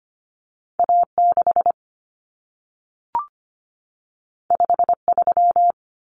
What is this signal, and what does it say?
Morse code, heard as a keyed tone, about 25 wpm, 700 Hz. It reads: A6 53
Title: Common Cut Numbers - 2-Digit 25wpm